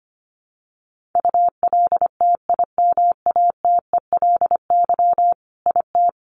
Morse code, heard as a keyed tone, about 25 wpm, 700 Hz. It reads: ULTIMATELY IT
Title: Sentences from Top 1200 Words 25wpm